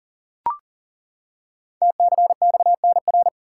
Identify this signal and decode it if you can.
Morse code, heard as a keyed tone, about 38 wpm, 690 Hz. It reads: TCXNR
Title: Code Groups - 5 Letters - 40wpm